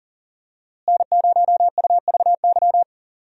Morse code, heard as a keyed tone, about 40 wpm, 700 Hz. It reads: N0UVY
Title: USA Call Signs - 5 Characters - Repeated 2x - 40wpm